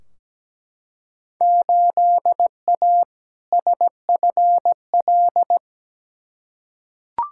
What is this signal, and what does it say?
Morse code, heard as a keyed tone, about 17 wpm, 700 Hz. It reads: 8A SFL
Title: Field Day Exchanges Encore - 17wpm